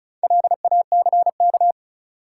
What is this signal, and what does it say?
Morse code, heard as a keyed tone, about 36 wpm, 700 Hz. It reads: LACK
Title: Sets of 3 Words - New Word Drill - 35wpm